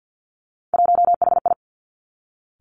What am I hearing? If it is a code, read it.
Morse code, heard as a keyed tone, about 50 wpm, 700 Hz. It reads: 1HI
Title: Three Challenging Letter-Number Combinations 50wpm